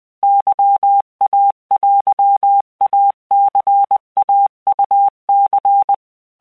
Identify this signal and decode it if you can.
Morse code (Japanese Wabun code), heard as a keyed tone, about 20 wpm, 800 Hz. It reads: ケイテイニイウニ
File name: keuikuchiyokugo20wptan